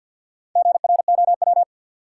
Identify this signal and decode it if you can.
Morse code, heard as a keyed tone, about 53 wpm, 700 Hz. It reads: GROW
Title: Sentences from Top 300 Words - New Word Drill - 50wpm